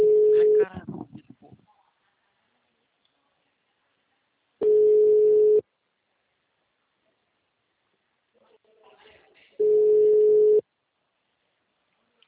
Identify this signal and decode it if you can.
Ringback tone (Europe, 425 Hz), with speech.
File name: out- -2018- - - .84